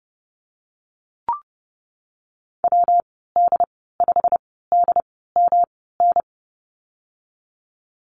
Morse code, heard as a keyed tone, about 30 wpm, 700 Hz. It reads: W D 5 D M N
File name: Lesson 19g - Letter w - Call Signs 30wpm